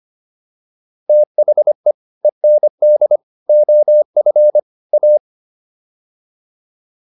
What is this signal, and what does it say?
Morse code, heard as a keyed tone, about 25 wpm, 600 Hz. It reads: THE END OF A